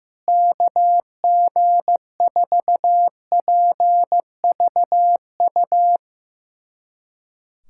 Morse code, heard as a keyed tone, about 15 wpm, 700 Hz. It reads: KG4PVU
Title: USA Callsign DE Callsign - 6 Characters - 15wpm